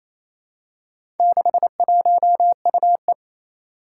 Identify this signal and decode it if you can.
Morse code, heard as a keyed tone, about 27 wpm, 700 Hz. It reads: 61UE
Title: Four Challenging Letter-Number Combinations 28wpm